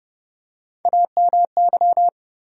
Morse code, heard as a keyed tone, about 29 wpm, 700 Hz. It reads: AMY
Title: USA Common Names - Max 3 Letters - 30wpm